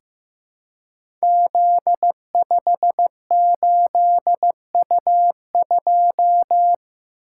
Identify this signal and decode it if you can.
Morse code, heard as a keyed tone, about 15 wpm, 700 Hz. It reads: Z58U2
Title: Code Groups - 5 Letter:Numbers - 15wpm